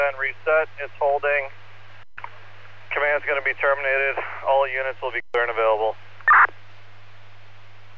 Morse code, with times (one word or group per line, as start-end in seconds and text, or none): none